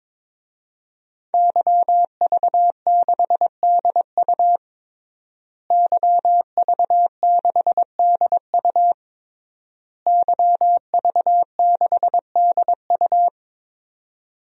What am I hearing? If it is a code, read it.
Morse code, heard as a keyed tone, about 22 wpm, 700 Hz. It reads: YV6DU YV6DU YV6DU
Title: DXCC-100 Call Signs - 5 Characters - Repeated 3x - 22wpm